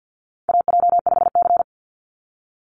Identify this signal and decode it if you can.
Morse code, heard as a keyed tone, about 50 wpm, 700 Hz. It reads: AJ5C
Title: NA SOTA Chaser Callsigns 50wpm